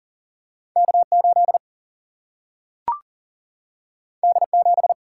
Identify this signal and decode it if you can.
Morse code, heard as a keyed tone, about 41 wpm, 700 Hz. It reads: K8 D7
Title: Single-Letter Single-Number 40wpm